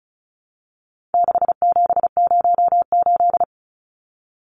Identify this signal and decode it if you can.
Morse code, heard as a keyed tone, about 34 wpm, 700 Hz. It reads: 6708
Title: Numbers - 4-Digit 35wpm